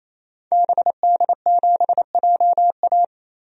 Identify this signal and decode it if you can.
Morse code, heard as a keyed tone, about 28 wpm, 700 Hz. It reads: BD7JA
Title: DXCC-100 Call Signs - 5 Characters - 28wpm